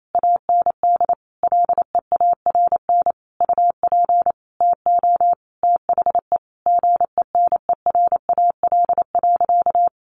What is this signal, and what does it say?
Morse code, heard as a keyed tone, about 28 wpm, 700 Hz. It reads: AND LEARN UP TO THE GENERAL.
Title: Sentences from Top 1500 Words 28wpm